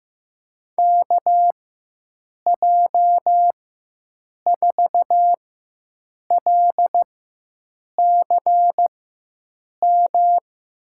Morse code, heard as a keyed tone, about 15 wpm, spaced as for 7 wpm, 700 Hz. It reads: KJ4LCM
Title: USA Call Signs - 6 Characters - 4x Character Spacing - 15wpm